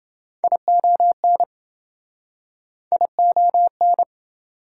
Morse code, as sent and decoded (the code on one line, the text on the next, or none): .. --- -. / .. --- -.
ION ION